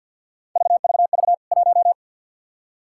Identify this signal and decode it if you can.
Morse code, heard as a keyed tone, about 51 wpm, 700 Hz. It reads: VVV 1
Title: K1USN SST QSO Practice 50wpm